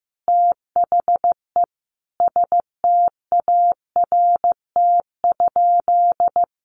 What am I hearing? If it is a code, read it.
Morse code, heard as a keyed tone, about 15 wpm, 700 Hz. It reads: THE START?
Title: Sentences from Top 1500 Words 15wpm